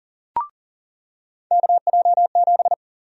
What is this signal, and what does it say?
Morse code, heard as a keyed tone, about 40 wpm, 700 Hz. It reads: KJ7
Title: USA Call Sign Prefixes - 40wpm